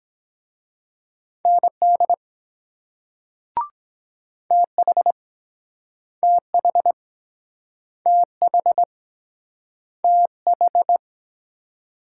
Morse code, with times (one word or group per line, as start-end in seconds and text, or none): 1.4-2.1 ND
4.5-5.1 TH
6.2-6.9 TH
8.1-8.8 TH
10.0-11.0 TH